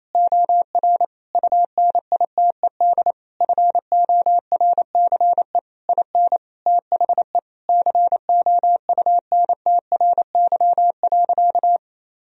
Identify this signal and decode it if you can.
Morse code, heard as a keyed tone, about 27 wpm, 700 Hz. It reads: OR UNITED FORCE IN THE COUNTRY.
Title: Sentences from Top 1000 Words - Review Part 4 - 28wpm